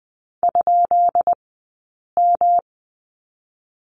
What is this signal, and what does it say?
Morse code, heard as a keyed tone, about 20 wpm, spaced as for 8 wpm, 700 Hz. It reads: ?M
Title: 171 Lesson 28d - Number 9 - Two Characters 20wpm